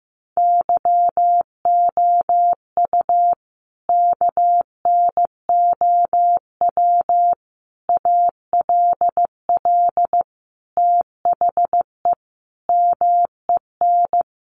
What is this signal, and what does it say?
Morse code, heard as a keyed tone, about 15 wpm, 700 Hz. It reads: YOU KNOW ALL THE MEN
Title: Sentences from Top 300 Words 15wpm